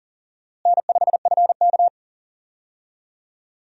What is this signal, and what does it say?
Morse code, heard as a keyed tone, about 40 wpm, 700 Hz. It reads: N5FK
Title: USA Call Signs - 4 Characters - Repeated 2x - 40wpm